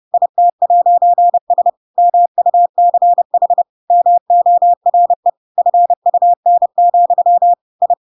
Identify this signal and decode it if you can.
Morse code, heard as a keyed tone, about 30 wpm, 700 Hz. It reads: IT'S MUCH MORE FUN, I
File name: Pinocchio / Pinocchio ch4 30 30 ik1hge